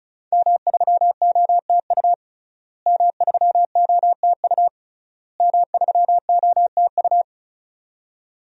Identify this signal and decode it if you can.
Morse code, heard as a keyed tone, about 34 wpm, 700 Hz. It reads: M3OTU M3OTU M3OTU
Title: DXCC-100 Call Signs - 5 Characters - Repeated 3x - 35wpm